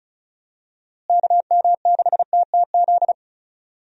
Morse code, heard as a keyed tone, about 36 wpm, 700 Hz. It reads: KM6TTZ
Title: USA Call Signs - 6 Characters - Repeated 2x - 35wpm